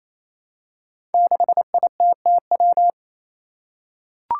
Morse code, heard as a keyed tone, about 28 wpm, 700 Hz. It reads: 6ITTW
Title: Five Challenging Letter-Number Combinations 28wpm